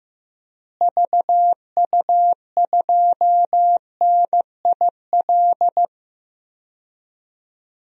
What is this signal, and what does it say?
Morse code, heard as a keyed tone, about 15 wpm, 700 Hz. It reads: VU2NIL
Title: DXCC-100 Call Signs - 6 Characters - Repeated 2x - 15wpm